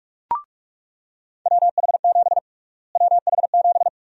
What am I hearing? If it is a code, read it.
Morse code, heard as a keyed tone, about 47 wpm, 700 Hz. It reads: WH7 WH7
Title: USA Call Sign Prefixes - Repeated 2x - 45wpm